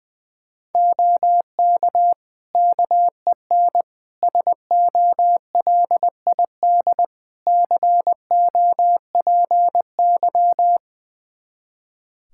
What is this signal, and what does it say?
Morse code, heard as a keyed tone, about 21 wpm, 700 Hz. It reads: OK KEN SOLID COPY
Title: Practice QSO - Exchange 3 - More Info - 20wpm